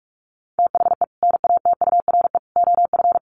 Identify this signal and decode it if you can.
Morse code, heard as a keyed tone, about 44 wpm, 700 Hz. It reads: THE NATURE OF
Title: Sets of 3 Words 45wpm